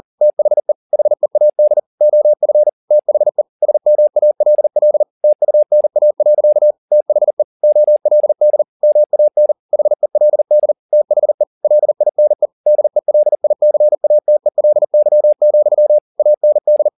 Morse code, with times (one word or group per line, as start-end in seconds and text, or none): 0.2-0.7 THE
0.9-1.8 HEAD
2.0-2.7 OF
2.9-3.4 THE
3.6-5.0 SMALL
5.2-6.7 TUNA.
6.9-7.4 THE
7.6-8.6 OLD
8.8-9.5 MAN
9.7-10.7 HELD
10.9-11.4 THE
11.6-12.5 LINE
12.7-16.0 DELICATELY,
16.2-16.9 AND